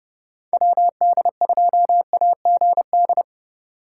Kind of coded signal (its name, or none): Morse code